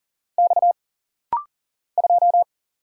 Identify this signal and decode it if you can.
Morse code, heard as a keyed tone, about 40 wpm, 700 Hz. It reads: X 2
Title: Single Letter-Number - Rapid-Fire 40wpm